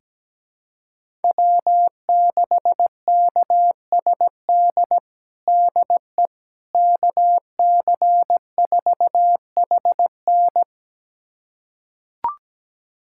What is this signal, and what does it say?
Morse code, heard as a keyed tone, about 17 wpm, 700 Hz. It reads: W6KSD DE KC4HN
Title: USA Callsign DE Callsign - 5 Characters - 17wpm